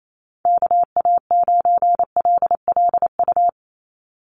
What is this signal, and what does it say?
Morse code, heard as a keyed tone, about 28 wpm, 700 Hz. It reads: KA9LLU
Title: USA Call Signs 28wpm